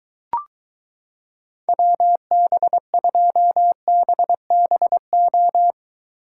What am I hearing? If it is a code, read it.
Morse code, heard as a keyed tone, about 23 wpm, 700 Hz. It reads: WB2BBO